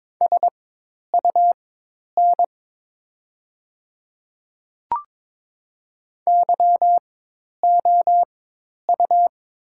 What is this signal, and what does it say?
Morse code, heard as a keyed tone, about 22 wpm, spaced as for 10 wpm, 700 Hz. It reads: SUN YOU
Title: 4x Character Spacing - Top 300 Words - 22wpm